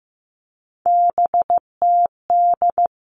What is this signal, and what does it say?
Morse code, heard as a keyed tone, about 15 wpm, 700 Hz. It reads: BTD
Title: All Cut Numbers - 3-Digit 15wpm